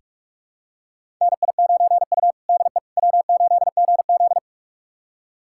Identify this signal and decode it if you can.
Morse code, heard as a keyed tone, about 44 wpm, 690 Hz. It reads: NI9U DE W8GZ
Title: USA Callsign DE Callsign - 4 Characters - 45wpm